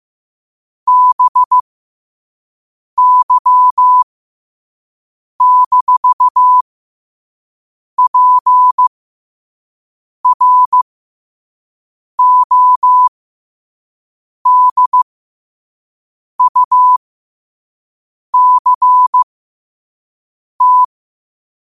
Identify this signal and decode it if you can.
Morse code, heard as a keyed tone, about 15 wpm, spaced as for 5 wpm, 1000 Hz. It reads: BY-PRODUCT
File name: qotdfiles / 5wpm